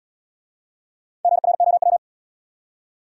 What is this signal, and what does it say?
Morse code, heard as a keyed tone, about 68 wpm, 700 Hz. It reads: GROW